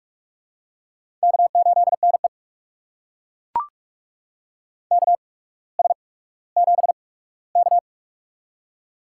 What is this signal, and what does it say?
Morse code, heard as a keyed tone, about 42 wpm, spaced as for 19 wpm, 700 Hz. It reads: K8NE KS7K
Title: USA Call Signs - 4 Characters - 8x Character Spacing - 45wpm